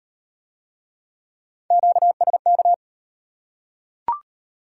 Morse code, heard as a keyed tone, about 33 wpm, 700 Hz. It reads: QSK